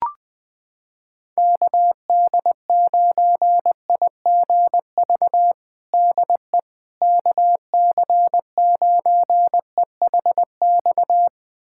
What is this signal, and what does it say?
Morse code, heard as a keyed tone, about 20 wpm, 700 Hz. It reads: KD9IGV DE KC9EHX
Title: USA Callsign DE Callsign - 6 Characters - 20wpm